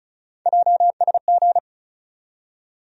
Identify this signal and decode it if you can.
Morse code, heard as a keyed tone, about 36 wpm, 700 Hz. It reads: JSG
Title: USA Call Sign Postfixes - 3 Letters 35wpm